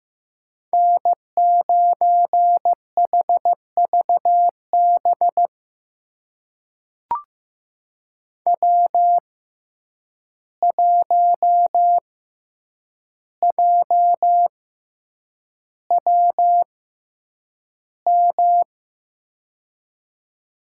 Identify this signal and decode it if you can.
Morse code, heard as a keyed tone, about 15 wpm, spaced as for 8 wpm, 700 Hz. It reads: N9HVB W1JWM
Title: USA Call Signs - 6x Character Spacing - 15wpm